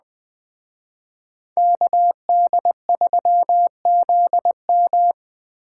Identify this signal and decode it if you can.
Morse code, heard as a keyed tone, about 20 wpm, 700 Hz. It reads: KD3ZM
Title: Speed Racing - USA 2x2 Call Signs 17wpm